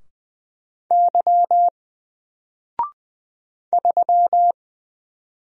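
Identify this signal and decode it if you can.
Morse code, heard as a keyed tone, about 20 wpm, 700 Hz. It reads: Y 3
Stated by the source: Single letter and number Morse drill at 20 wpm, from N4LCD.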